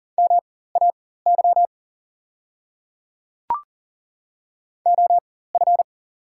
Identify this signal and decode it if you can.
Morse code, heard as a keyed tone, about 40 wpm, spaced as for 19 wpm, 690 Hz. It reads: MAY OF
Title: 4x Character Spacing - Top 100 Words - 40wpm